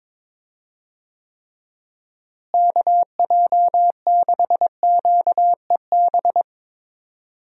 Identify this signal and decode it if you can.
Morse code, heard as a keyed tone, about 22 wpm, 700 Hz. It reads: KJ6QEB
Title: USA Call Signs - 6 Characters - Repeated 2x - 22wpm